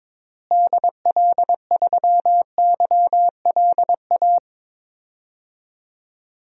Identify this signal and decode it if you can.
Morse code, heard as a keyed tone, about 22 wpm, 700 Hz. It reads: DL3YLA
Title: DXCC-100 Call Signs - 6 Characters - Repeated 2x - 22wpm